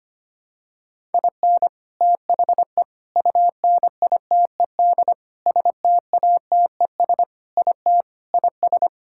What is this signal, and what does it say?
Morse code, heard as a keyed tone, about 25 wpm, 700 Hz. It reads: IN THE UNITED STATES IT IS
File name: practice / 25wpm / Sentences with 3-Word Phrases - 25wpm